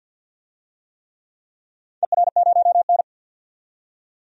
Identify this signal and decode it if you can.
Morse code, heard as a keyed tone, about 53 wpm, 700 Hz. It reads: ER0N
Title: DXCC-100 Call Signs - 4 Characters - 50wpm